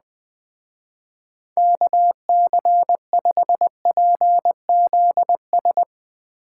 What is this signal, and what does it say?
Morse code, heard as a keyed tone, about 19 wpm, 700 Hz. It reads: KC5PZS
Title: Speed Racing - USA 2x3 Call Signs 17wpm